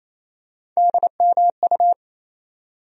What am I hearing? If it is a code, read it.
Morse code, heard as a keyed tone, about 28 wpm, 700 Hz. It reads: DMU
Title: USA Call Sign Postfixes - 3 Letters 28wpm